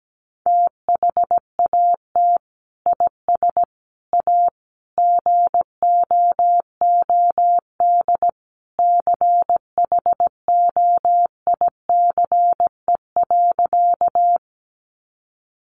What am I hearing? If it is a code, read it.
Morse code, heard as a keyed tone, about 17 wpm, 700 Hz. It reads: THAT IS A GOOD CHOICE.
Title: Sentences from Top 900 Words 17wpm